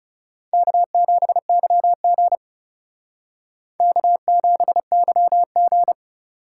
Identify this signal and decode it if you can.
Morse code, heard as a keyed tone, about 37 wpm, 700 Hz. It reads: K7YG K7YG
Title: Speed Racing - USA Call Signs 30wpm